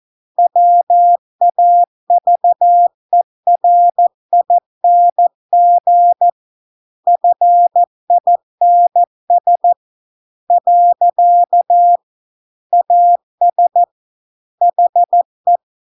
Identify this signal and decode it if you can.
Morse code, heard as a keyed tone, about 14 wpm, 700 Hz. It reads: WAVERING FINS . AS HE